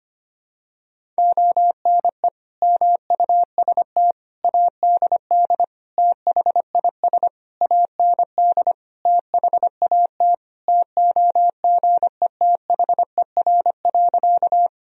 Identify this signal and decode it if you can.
Morse code, heard as a keyed tone, about 25 wpm, 700 Hz. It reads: ONE MUST ADD THIS AND THAT TOGETHER.
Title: Sentences from Top 1000 Words - Review Part 1 - 25wpm